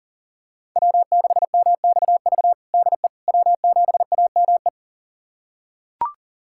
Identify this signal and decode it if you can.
Morse code, heard as a keyed tone, about 40 wpm, 700 Hz. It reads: W6MXV DE W7AME
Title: USA Callsign DE Callsign - 5 Characters - 40wpm